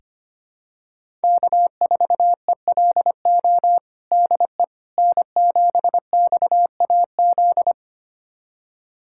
Morse code, heard as a keyed tone, about 25 wpm, 700 Hz. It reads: K4ELO DE N7XAZ
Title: USA Callsign DE Callsign 25wpm